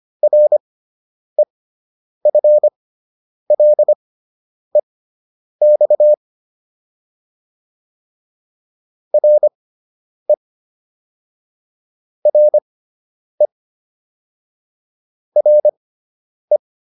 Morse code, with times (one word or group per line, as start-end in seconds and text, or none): 0.2-6.1 REFLEX
9.1-10.3 RE
12.3-13.5 RE
15.4-16.6 RE